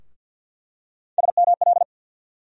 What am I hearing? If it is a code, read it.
Morse code, heard as a keyed tone, about 51 wpm, 700 Hz. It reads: SML
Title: USA Call Sign Postfixes - 3 Letters 50wpm